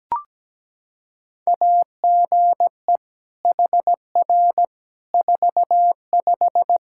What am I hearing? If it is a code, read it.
Morse code, heard as a keyed tone, about 17 wpm, 700 Hz. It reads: AGE HR 45
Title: Practice QSO - Exchange 3 - More Info - 17wpm